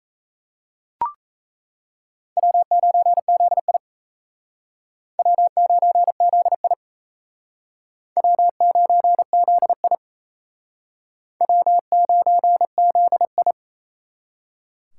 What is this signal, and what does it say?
Morse code, heard as a keyed tone, about 34 wpm, 700 Hz. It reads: W9ZI W9ZI W9ZI W9ZI
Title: Speed Racing - USA 1x2 Call Signs 28wpm